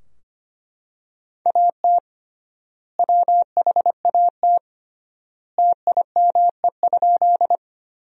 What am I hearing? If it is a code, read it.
Morse code, heard as a keyed tone, about 25 wpm, 700 Hz. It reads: AT WHAT TIME?
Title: 3x Word-Spacing - Sentences from Top 100 Words - 25wpm